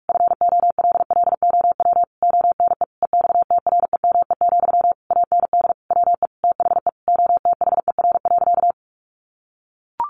Morse code, heard as a keyed tone, about 44 wpm, 700 Hz. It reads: FOLLOW ONE EXTREME, AND WE THE OTHER.
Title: Sentences from Top 1300 Words 45wpm